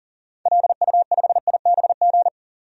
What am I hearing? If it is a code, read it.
Morse code, heard as a keyed tone, about 40 wpm, 700 Hz. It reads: LU5IBG
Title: DXCC-100 Call Signs - 6 Characters - 40wpm